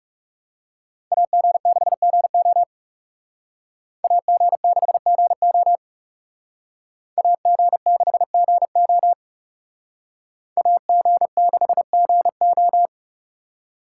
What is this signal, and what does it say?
Morse code, heard as a keyed tone, about 38 wpm, 700 Hz. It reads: AG6GO AG6GO AG6GO AG6GO